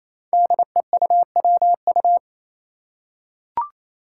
Morse code, heard as a keyed tone, about 28 wpm, 700 Hz. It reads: DEUWU
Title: Five Challenging Letter-Number Combinations 28wpm